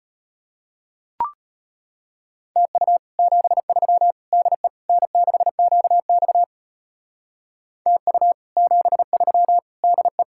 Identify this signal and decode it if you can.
Morse code, heard as a keyed tone, about 29 wpm, 700 Hz. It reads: TU 73 DE N6QX TU 73 DE
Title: Speed-Racing Common QSO Phrases Encore - 25wpm